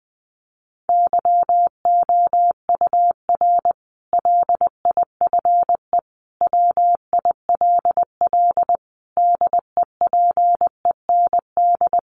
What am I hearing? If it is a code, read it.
Morse code, heard as a keyed tone, about 20 wpm, 700 Hz. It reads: YOUR LIFE WILL DEPEND
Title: Sentences from Top 1300 Words 20wpm